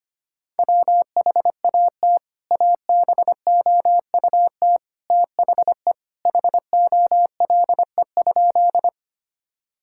Morse code, heard as a keyed tone, about 25 wpm, 700 Hz. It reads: WHAT ABOUT THE HOLE?
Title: Sentences from Top 1600 Words 25wpm